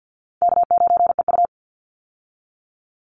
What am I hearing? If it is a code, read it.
Morse code, heard as a keyed tone, about 49 wpm, 700 Hz. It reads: K9EU
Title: Easy Call Signs Repeated 2x - 50wpm